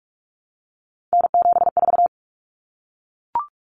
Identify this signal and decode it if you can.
Morse code, heard as a keyed tone, about 44 wpm, 700 Hz. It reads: N74
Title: Common Cut Numbers - 3-Digit 45wpm